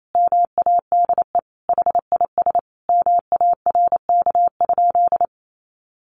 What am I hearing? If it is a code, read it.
Morse code, heard as a keyed tone, about 28 wpm, 700 Hz. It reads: MADE HIS MARK?
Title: Sentences from Top 300 Words 28wpm